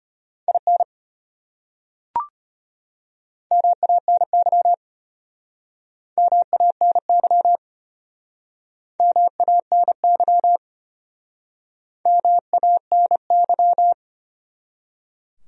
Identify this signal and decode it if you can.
Morse code, heard as a keyed tone, about 33 wpm, 700 Hz. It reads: IN MANY MANY MANY MANY